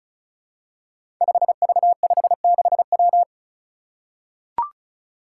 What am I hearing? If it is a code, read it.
Morse code, heard as a keyed tone, about 36 wpm, 700 Hz. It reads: 5V56W
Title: Five Challenging Letter-Number Combinations 35wpm